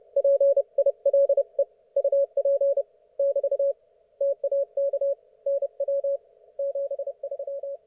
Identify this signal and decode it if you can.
Morse code, heard as a keyed tone, about 30 wpm, 560 Hz. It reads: PILE UP <BT> TAK NW 73
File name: Archiv / CUC179 / ok1fhi cul179